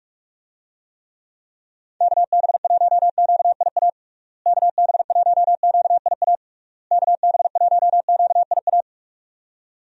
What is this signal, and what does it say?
Morse code, heard as a keyed tone, about 46 wpm, 700 Hz. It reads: KB1QIA KB1QIA KB1QIA
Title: USA Call Signs - 6 Characters - Repeated 3x - 45wpm